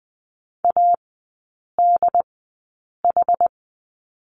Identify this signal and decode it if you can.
Morse code, heard as a keyed tone, about 20 wpm, spaced as for 8 wpm, 700 Hz. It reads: ADH
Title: 121 Lesson 20e - Number 3 - Three Characters 20wpm